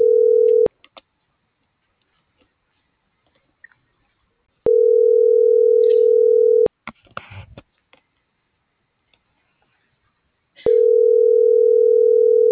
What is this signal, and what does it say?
Ringback tone (North America), 440 + 480 Hz.